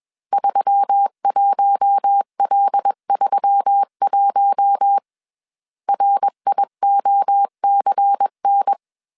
Morse code, heard as a keyed tone, about 21 wpm, 800 Hz. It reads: 31L31 RIOCN
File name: CW / DET / 1 6 1500C S C 20wpm 2116